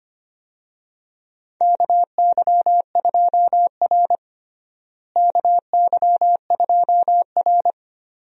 Morse code, heard as a keyed tone, about 25 wpm, 700 Hz. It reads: KY2R KY2R
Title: USA Call Signs - 4 Characters - Repeated 3x - 25wpm